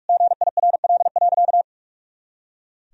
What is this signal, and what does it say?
Morse code, heard as a keyed tone, about 46 wpm, 700 Hz. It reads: GIRL.
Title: Sentences from Top 400 Words and Abbreviations 45wpm